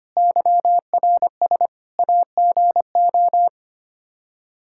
Morse code, heard as a keyed tone, about 25 wpm, 700 Hz. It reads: YRS AGO
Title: Practice QSO - Exchange 3 - More Info - 25wpm